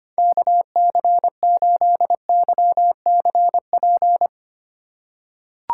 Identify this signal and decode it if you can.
Morse code, heard as a keyed tone, about 25 wpm, 700 Hz. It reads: KC8YCP
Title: USA Call Signs - 4x Character Spacing - 25wpm